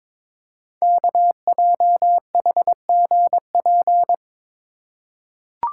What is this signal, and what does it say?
Morse code, heard as a keyed tone, about 23 wpm, 700 Hz. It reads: KJHGP
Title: Code Groups - 5 Letters - 22wpm